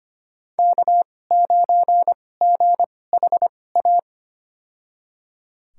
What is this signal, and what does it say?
Morse code, heard as a keyed tone, about 25 wpm, spaced as for 18 wpm, 700 Hz. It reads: K9GHA
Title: USA Call Signs - 5 Characters - 2x Character Spacing - 25wpm